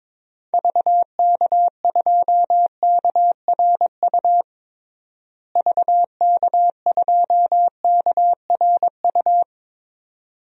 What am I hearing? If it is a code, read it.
Morse code, heard as a keyed tone, about 22 wpm, 700 Hz. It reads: VK2KRU VK2KRU
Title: DXCC-100 Call Signs - 6 Characters - Repeated 2x - 22wpm